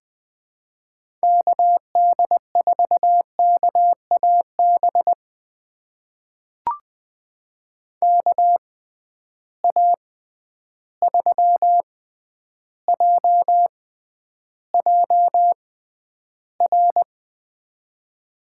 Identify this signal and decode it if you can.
Morse code, heard as a keyed tone, about 20 wpm, spaced as for 10 wpm, 700 Hz. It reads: KD4KAB KA3JJR
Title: USA Call Signs - 6 Characters - 6x Character Spacing - 20wpm